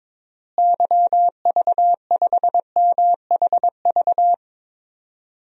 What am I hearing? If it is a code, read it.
Morse code, heard as a keyed tone, about 22 wpm, 700 Hz. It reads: YV5MHV
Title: DXCC-100 Call Signs - 6 Characters - Repeated 3x - 22wpm